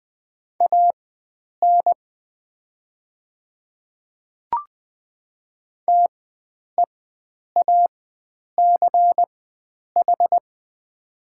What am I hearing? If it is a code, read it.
Morse code, heard as a keyed tone, about 20 wpm, spaced as for 9 wpm, 700 Hz. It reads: AN TEACH
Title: 4x Character Spacing - Top 500 Words - 20wpm